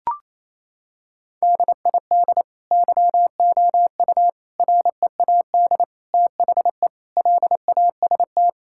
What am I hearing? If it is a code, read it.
Morse code, heard as a keyed tone, about 28 wpm, 700 Hz. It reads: DID YOU READ THE LAST
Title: Sentences from Top 1000 Words - Review Part 5 - 28wpm